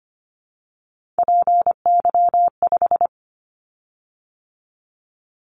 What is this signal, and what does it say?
Morse code, heard as a keyed tone, about 25 wpm, 700 Hz. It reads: PY5
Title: Callsign Prefixes for Top 50 DXCC - 25wpm